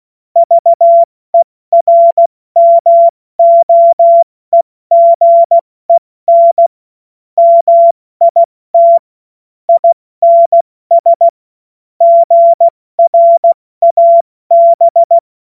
Morse code, heard as a keyed tone, about 16 wpm, 680 Hz. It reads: VERMOEGEN MIT INS GRAB
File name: Morsekurs / Witze / Witz 06. September 2022 70 F80